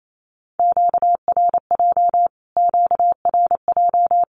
Morse code, heard as a keyed tone, about 28 wpm, 700 Hz. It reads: QRJ QRJ
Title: Q-Codes Repeated 3x 28wpm